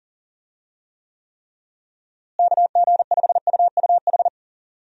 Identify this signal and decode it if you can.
Morse code, heard as a keyed tone, about 40 wpm, 700 Hz. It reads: KG5UUH
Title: USA Call Signs - 6 Characters - Repeated 3x - 40wpm